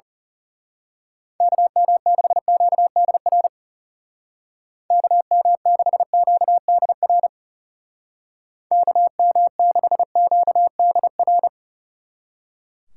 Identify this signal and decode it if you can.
Morse code, heard as a keyed tone, about 34 wpm, 700 Hz. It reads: KM6QDR KM6QDR KM6QDR
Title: Speed Racing - USA 2x3 Call Signs 30wpm